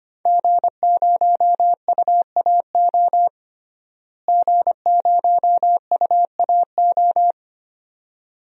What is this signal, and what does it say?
Morse code, heard as a keyed tone, about 25 wpm, 700 Hz. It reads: G0UAO G0UAO
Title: DXCC-100 Call Signs - 5 Characters - Repeated 2x - 25wpm